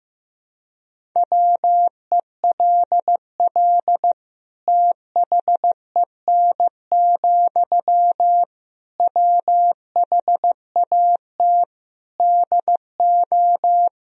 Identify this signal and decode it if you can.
Morse code, heard as a keyed tone, about 15 wpm, 700 Hz. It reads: WELL THEN, WHAT DO
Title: Sentences from Top 200 Words and Abbreviations 15wpm